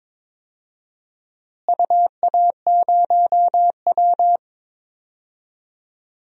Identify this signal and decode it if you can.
Morse code, heard as a keyed tone, about 22 wpm, 700 Hz. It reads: UA0W
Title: DXCC-100 Call Signs - 4 Characters - Repeated 2x - 22wpm